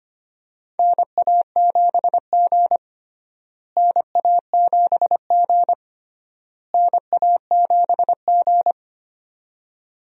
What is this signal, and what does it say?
Morse code, heard as a keyed tone, about 25 wpm, 700 Hz. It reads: NA7G NA7G NA7G
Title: USA Call Signs - 4 Characters - Repeated 3x - 25wpm